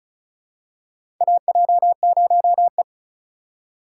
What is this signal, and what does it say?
Morse code, heard as a keyed tone, about 36 wpm, 700 Hz. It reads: AJ0E
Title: USA Call Signs - 4 Characters - 3x Character Spacing - 35wpm